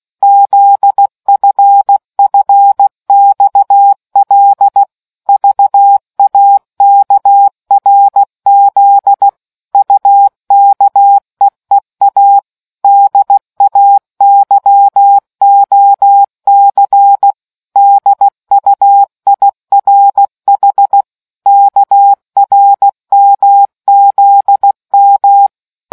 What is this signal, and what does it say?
Morse code, heard as a keyed tone, about 16 wpm, 800 Hz. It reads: ZFFXL VAKRZ UKEEA DAYOC DUIRH KRMZM